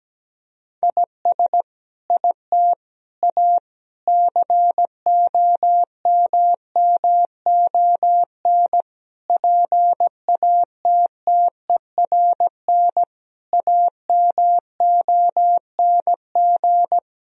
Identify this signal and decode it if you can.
Morse code, heard as a keyed tone, about 17 wpm, 700 Hz. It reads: IS IT A COMMON PATTERN AMONG
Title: Sentences from Top 1000 Words - Review Part 2 - 17wpm